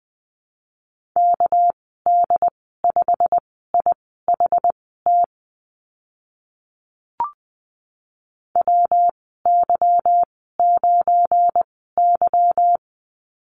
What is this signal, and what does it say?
Morse code, heard as a keyed tone, about 20 wpm, spaced as for 14 wpm, 700 Hz. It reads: KD5IHT WY9Y
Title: 238 Lesson 38g - Letter q - Call Signs 20wpm